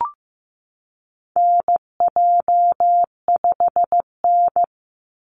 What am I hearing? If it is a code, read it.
Morse code, heard as a keyed tone, about 15 wpm, 700 Hz. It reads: NJ5N
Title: Call Signs Repeated 3x - 15wpm